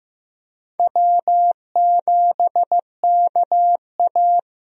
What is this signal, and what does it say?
Morse code, heard as a keyed tone, about 15 wpm, 700 Hz. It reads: W7KA
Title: USA Call Signs - 4 Characters - Repeated 2x - 15wpm